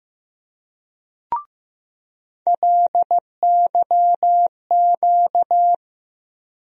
Morse code, heard as a keyed tone, about 16 wpm, 700 Hz. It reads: LYQ
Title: USA Call Sign Postfixes - 3 Letters 15wpm